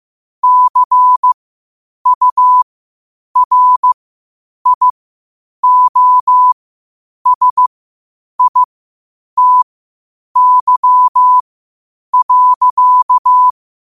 Morse code, heard as a keyed tone, about 15 wpm, spaced as for 8 wpm, 1000 Hz. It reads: CURIOSITY.
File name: qotdfiles / 8wpm